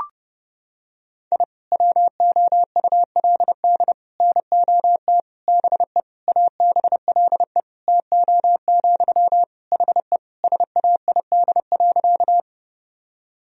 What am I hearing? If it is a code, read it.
Morse code, heard as a keyed tone, about 30 wpm, 700 Hz. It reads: I WOULD NOT BE ABLE TO, HE SAID.
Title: Sentences from Top 1000 Words - Review Part 2 - 30wpm